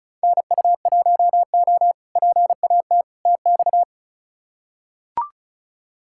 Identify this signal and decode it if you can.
Morse code, heard as a keyed tone, about 36 wpm, 700 Hz. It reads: NU1O PAT TX
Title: K1USN SST QSO Practice - 35wpm